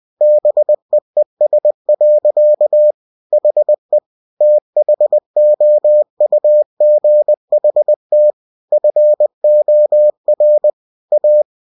Morse code, heard as a keyed tone, about 20 wpm, 600 Hz. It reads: BEES. HE THOUGHT FOR A
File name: cw-books / winnie-the-pooh / winnie-20wpm-page-0006